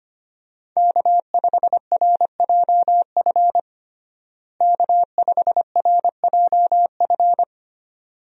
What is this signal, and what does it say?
Morse code, heard as a keyed tone, about 25 wpm, 700 Hz. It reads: K5RJF K5RJF
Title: USA Call Signs - 5 Characters - Repeated 2x - 25wpm